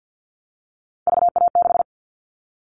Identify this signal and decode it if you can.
Morse code, heard as a keyed tone, about 50 wpm, 700 Hz. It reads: VA6
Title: Callsign Prefixes for Top 50 DXCC - 50wpm